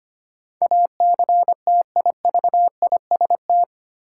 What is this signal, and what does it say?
Morse code, heard as a keyed tone, about 25 wpm, 700 Hz. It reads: ACTIVIST
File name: practice / 25wpm / 8 Letter Words - First 3 Letters Given - 25wpm